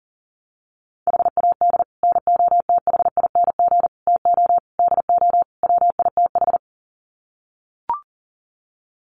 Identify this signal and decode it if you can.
Morse code, heard as a keyed tone, about 40 wpm, 700 Hz. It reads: HAD NOTHING TO DO WITH
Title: Sets of 5 Words 40wpm